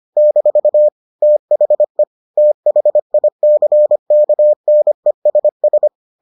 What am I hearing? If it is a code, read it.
Morse code, heard as a keyed tone, about 25 wpm, 600 Hz. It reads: - THE THICKNESS